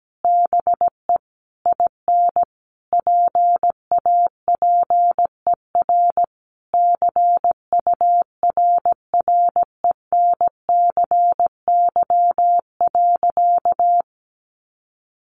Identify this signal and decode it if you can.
Morse code, heard as a keyed tone, about 17 wpm, 700 Hz. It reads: BE IN PAPER CURRENCY.
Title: Sentences from Top 1500 Words 17wpm